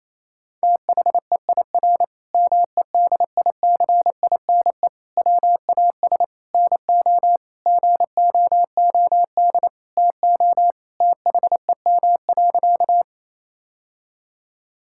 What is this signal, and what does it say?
Morse code, heard as a keyed tone, about 27 wpm, 700 Hz. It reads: THEIR MEDICINE WAS NO GOOD TO THEM.
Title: Sentences from Top 1000 Words - Review Part 6 - 28wpm